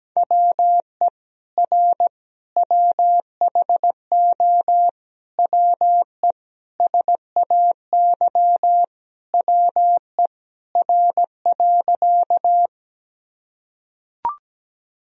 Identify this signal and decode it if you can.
Morse code, heard as a keyed tone, about 17 wpm, 700 Hz. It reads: WE R WHO WE SAY WE R.
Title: Sentences from Top 200 Words and Abbreviations 17wpm